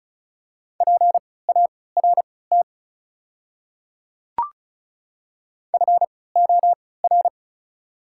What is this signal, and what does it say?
Morse code, heard as a keyed tone, about 36 wpm, spaced as for 20 wpm, 690 Hz. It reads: PART FOR
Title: 3x Character Spacing - Top 500 Words - 35wpm